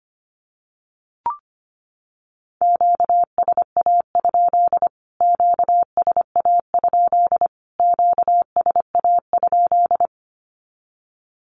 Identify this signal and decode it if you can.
Morse code, heard as a keyed tone, about 25 wpm, 700 Hz. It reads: QSA? QSA? QSA?
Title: Q-Codes Repeated 3x 25wpm